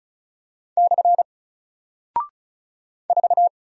Morse code, heard as a keyed tone, about 36 wpm, 700 Hz. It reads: / 4
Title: Single Character (Letter:Number:Punctuation:Prosign) Alt 35wpm